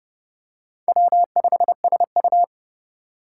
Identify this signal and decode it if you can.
Morse code, heard as a keyed tone, about 27 wpm, 700 Hz. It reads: W5SU